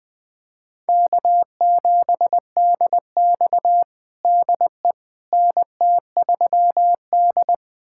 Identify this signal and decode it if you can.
Morse code, heard as a keyed tone, about 21 wpm, 700 Hz. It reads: K7DX DE NT3D
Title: USA Callsign DE Callsign - 4 Characters - 20wpm